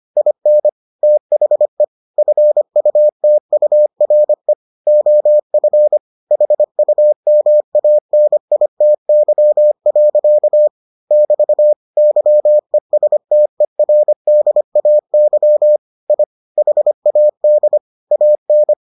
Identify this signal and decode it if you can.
Morse code, heard as a keyed tone, about 25 wpm, 600 Hz. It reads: IN THE FUTURE OF HUMANITY. <BT> YESTERDAY I HAD AN